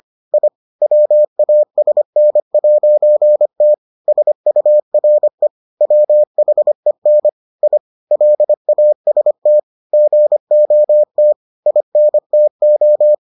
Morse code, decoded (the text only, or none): I WASN'T SURE WHEN I LAST GOT INTO